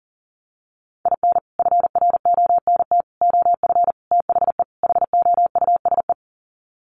Morse code, heard as a keyed tone, about 40 wpm, 700 Hz. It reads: IN FRONT OF THE HOUSE
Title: Sets of 5 Words Encore2 40wpm